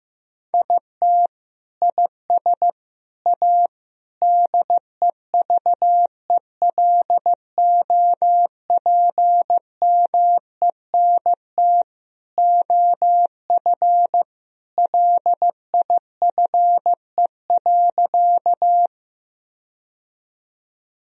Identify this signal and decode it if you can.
Morse code, heard as a keyed tone, about 15 wpm, 700 Hz. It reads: IT IS A DEVELOPMENT OF LIFE.
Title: Sentences from Top 1000 Words - Review Part 4 - 15wpm